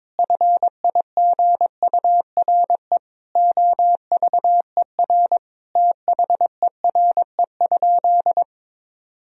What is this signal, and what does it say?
Morse code, heard as a keyed tone, about 22 wpm, 700 Hz. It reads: FIGURE OVER THERE?
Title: Sentences from Top 1000 Words - Review Part 2 - 22wpm